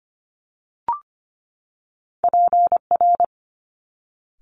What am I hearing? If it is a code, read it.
Morse code, heard as a keyed tone, about 25 wpm, 700 Hz. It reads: PR
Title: Top Two Letter Combinations in Words 25wpm